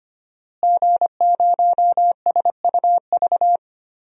Morse code, heard as a keyed tone, about 25 wpm, 700 Hz. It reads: G0SUV
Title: DXCC-100 Call Signs - 5 Characters - Repeated 2x - 25wpm